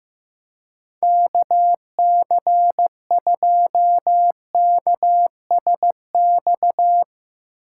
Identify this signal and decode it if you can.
Morse code, heard as a keyed tone, about 15 wpm, 700 Hz. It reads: KC2KSX
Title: USA Call Signs - 3x Character Spacing - 15wpm